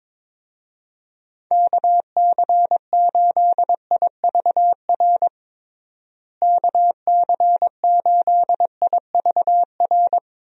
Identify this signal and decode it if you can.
Morse code, heard as a keyed tone, about 22 wpm, 700 Hz. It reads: KC8IVR KC8IVR
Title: USA Call Signs - 6 Characters - Repeated 2x - 22wpm